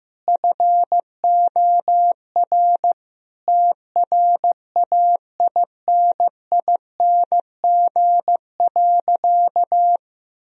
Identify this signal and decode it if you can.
Morse code, heard as a keyed tone, about 16 wpm, 700 Hz. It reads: FOR TRAINING.
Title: Sentences from Top 1000 Words - Review Part 4 - 15wpm